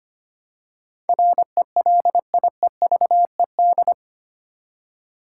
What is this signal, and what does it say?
Morse code, heard as a keyed tone, about 25 wpm, 700 Hz. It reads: RELIEVED
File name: Eight Letter Words - 25wpm